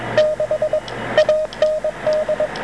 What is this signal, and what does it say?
Morse code, heard as a keyed tone, about 22 wpm, 620 Hz, with radio noise. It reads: 6 AND